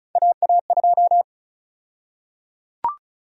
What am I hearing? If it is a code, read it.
Morse code, heard as a keyed tone, about 36 wpm, 700 Hz. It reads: AA2
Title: USA Call Sign Prefixes - 35wpm